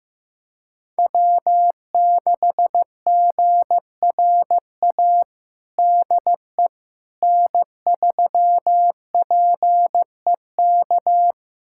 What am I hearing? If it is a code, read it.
Morse code, heard as a keyed tone, about 15 wpm, 700 Hz. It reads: W6GRA DE N3PEK
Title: USA Callsign DE Callsign 15wpm